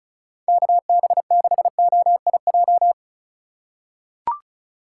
Morse code, heard as a keyed tone, about 36 wpm, 700 Hz. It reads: KB6OIJ
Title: USA Call Signs - 6 Characters - 2x Character Spacing - 35wpm